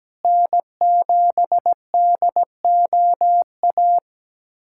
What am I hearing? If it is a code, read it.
Morse code, heard as a keyed tone, about 17 wpm, 700 Hz. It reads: N7DOA
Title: USA Callsign DE Callsign - 5 Characters - 17wpm